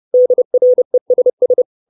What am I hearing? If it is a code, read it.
Morse code, heard as a keyed tone, about 30 wpm, 500 Hz. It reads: DRESS
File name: 632 Words 30WPM